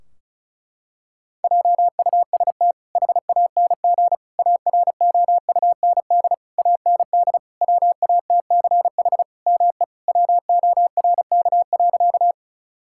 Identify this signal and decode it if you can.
Morse code, heard as a keyed tone, about 36 wpm, 700 Hz. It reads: JUST HANG AROUND AND WATCH ME WORK.
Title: Sentences from Top 1600 Words 35wpm